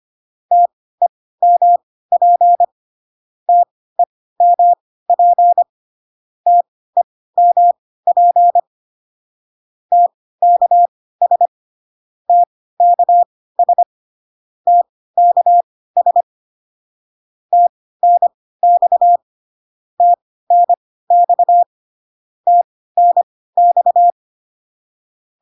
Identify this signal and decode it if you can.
Morse code, heard as a keyed tone, about 25 wpm, spaced as for 16 wpm, 700 Hz. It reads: TEMP TEMP TEMP TKS TKS TKS TNX TNX TNX